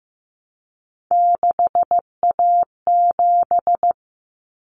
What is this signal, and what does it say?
Morse code, heard as a keyed tone, about 15 wpm, 700 Hz. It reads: 6A7
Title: Common Cut Numbers - 3-Digit 15wpm